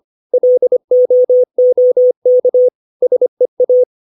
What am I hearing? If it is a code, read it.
Morse code, heard as a keyed tone, about 25 wpm, 500 Hz. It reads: LOOK SEA
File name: 632 words / 632 Words 25WPM